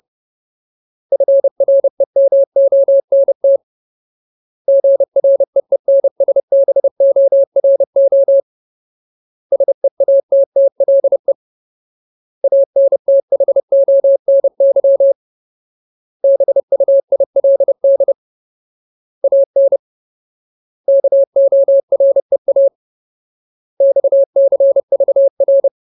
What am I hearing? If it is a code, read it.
Morse code, heard as a keyed tone, about 30 wpm, 550 Hz. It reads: FREMONT GREENSBORO SEATTLE ANTHONY BUILD AN KOREA XCVR
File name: Ultimate Word List / 30WPM Ultimate